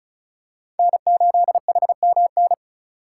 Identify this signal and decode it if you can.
Morse code, heard as a keyed tone, about 36 wpm, 700 Hz. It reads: N8HMN
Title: USA Call Signs - 5 Characters - 3x Character Spacing - 35wpm